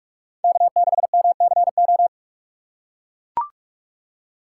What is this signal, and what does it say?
Morse code, heard as a keyed tone, about 43 wpm, 700 Hz. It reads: K6MCO
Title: USA Call Signs - 5 Characters - 6x Character Spacing - 45wpm